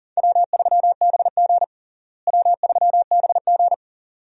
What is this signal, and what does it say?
Morse code, heard as a keyed tone, about 37 wpm, 700 Hz. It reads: W3BG W3BG
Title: USA Call Signs - 4 Characters - Repeated 2x - 40wpm